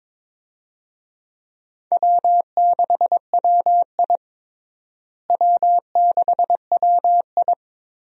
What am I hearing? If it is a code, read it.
Morse code, heard as a keyed tone, about 22 wpm, 700 Hz. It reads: W6WI W6WI
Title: USA Call Signs - 4 Characters - Repeated 2x - 22wpm